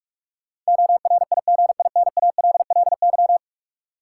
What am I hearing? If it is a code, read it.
Morse code, heard as a keyed tone, about 46 wpm, 700 Hz. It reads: ORIGINALLY
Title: Sentences from Top 1500 Words - New Word Drill - 45wpm